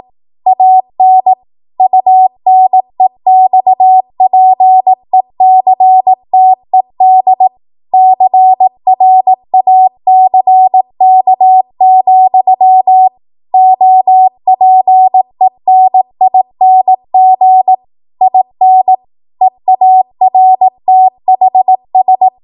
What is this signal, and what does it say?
Morse code, heard as a keyed tone, about 18 wpm, 750 Hz. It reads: AN UNEXPECTED CRACK, OPENING IN EARTHS
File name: Archive / 18 WPM / 18WPM